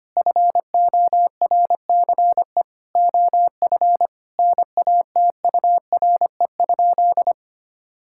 Morse code, heard as a keyed tone, about 25 wpm, 700 Hz. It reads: FORCE OF NATURE?
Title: Sentences from Top 1000 Words - Review Part 5 - 25wpm